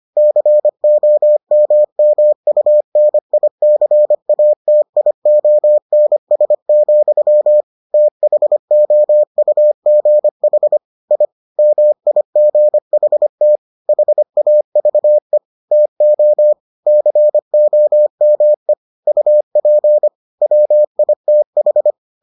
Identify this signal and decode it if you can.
Morse code, heard as a keyed tone, about 25 wpm, 600 Hz. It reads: COMMUNICATIONS, THOUGH I MIGHT HAVE TO COME UP WITH